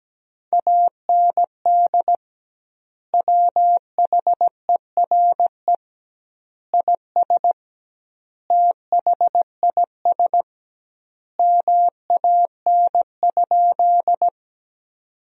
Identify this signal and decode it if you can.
Morse code, heard as a keyed tone, about 17 wpm, 700 Hz. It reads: AND WHERE IS THIS MAN?
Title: 2x Word-Spacing - Sentences from Top 200 Words - 17wpm